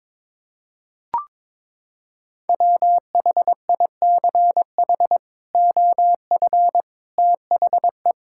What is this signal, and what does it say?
Morse code, heard as a keyed tone, about 22 wpm, 700 Hz. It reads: WHICH OF THE
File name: practice / 22wpm / Sentences with 8-Word Phrases - 22wpm